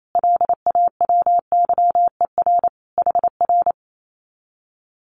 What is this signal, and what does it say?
Morse code, heard as a keyed tone, about 28 wpm, 700 Hz. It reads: LAWYER HR
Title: Common QSO Phrases Encore 28wpm